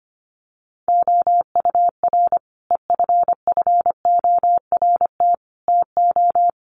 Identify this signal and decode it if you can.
Morse code, heard as a keyed tone, about 25 wpm, 700 Hz. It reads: OUR EFFORT TO
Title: Sentences from Top 1500 Words 25wpm